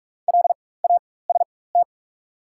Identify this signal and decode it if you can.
Morse code, heard as a keyed tone, about 47 wpm, spaced as for 21 wpm, 690 Hz. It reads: LAST
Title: 4x Character Spacing - Top 300 Words - 45wpm